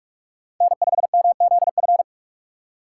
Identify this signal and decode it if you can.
Morse code, heard as a keyed tone, about 46 wpm, 700 Hz. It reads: N5MZF
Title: USA Call Signs - 5 Characters - 2x Character Spacing - 45wpm